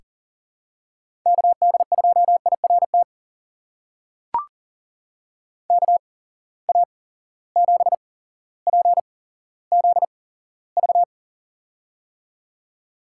Morse code, heard as a keyed tone, about 40 wpm, spaced as for 17 wpm, 700 Hz. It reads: KD2IRT KA7PZV
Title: USA Call Signs - 6 Characters - 8x Character Spacing - 40wpm